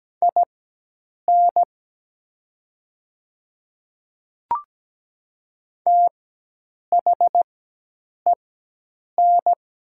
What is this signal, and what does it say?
Morse code, heard as a keyed tone, about 17 wpm, spaced as for 8 wpm, 700 Hz. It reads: IN THEN
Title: 4x Character Spacing - Top 100 Words - 17wpm